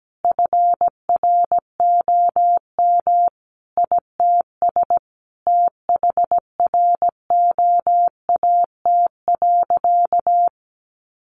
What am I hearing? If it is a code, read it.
Morse code, heard as a keyed tone, about 17 wpm, 700 Hz. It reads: FROM ITS THROAT.
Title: A Sound Of Thunder 17wpm